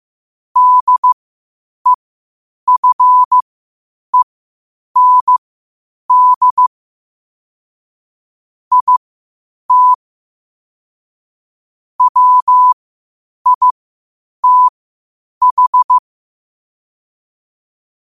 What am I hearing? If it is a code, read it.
Morse code, heard as a keyed tone, about 15 wpm, spaced as for 8 wpm, 1000 Hz. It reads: DEFEND IT WITH
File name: qotdfiles / 8wpm